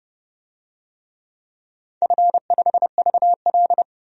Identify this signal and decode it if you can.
Morse code, heard as a keyed tone, about 31 wpm, 700 Hz. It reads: F5VL
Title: DXCC-100 Call Signs - 4 Characters - 30wpm